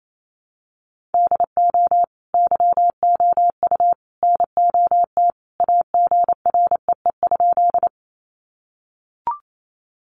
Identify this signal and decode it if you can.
Morse code, heard as a keyed tone, about 28 wpm, 700 Hz. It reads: DO YOU NOT AGREE?
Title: Sentences from Top 1100 Words 28wpm